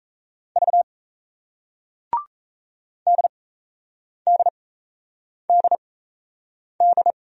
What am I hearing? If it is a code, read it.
Morse code, heard as a keyed tone, about 36 wpm, spaced as for 8 wpm, 700 Hz. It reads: V DDDD